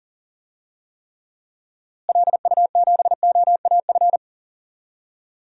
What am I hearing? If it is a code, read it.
Morse code, heard as a keyed tone, about 40 wpm, 700 Hz. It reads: LU7OAF
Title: DXCC-100 Call Signs - 6 Characters - Repeated 2x - 40wpm